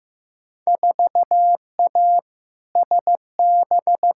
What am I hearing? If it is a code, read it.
Morse code, heard as a keyed tone, about 16 wpm, 700 Hz. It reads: 4A SB
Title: Field Day Exchanges Encore - 15wpm